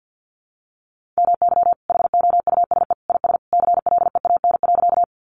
Morse code, heard as a keyed tone, about 50 wpm, 700 Hz. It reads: MY HOUSE IS CLEAN.
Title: Sentences from Top 1100 Words 50wpm